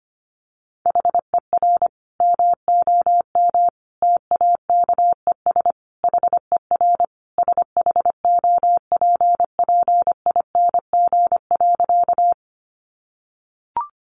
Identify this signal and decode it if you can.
Morse code, heard as a keyed tone, about 25 wpm, 700 Hz. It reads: HER MOM TAKES HER SHOPPING.
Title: Additional Easy Phrases 25wpm